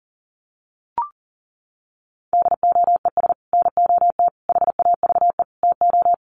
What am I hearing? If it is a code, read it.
Morse code, heard as a keyed tone, about 40 wpm, 700 Hz. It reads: DOES NOT HAVE TO
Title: Sets of 4 Words 40wpm